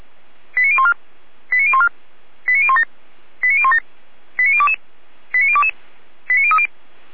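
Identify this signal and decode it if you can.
Five-tone selective call (ZVEI), heard as quick runs of five tones, tones of about 70 ms. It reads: 89014 89014 89017 89017 89022 89022 89030
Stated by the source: ZVEI five-tone selective call received by SDR.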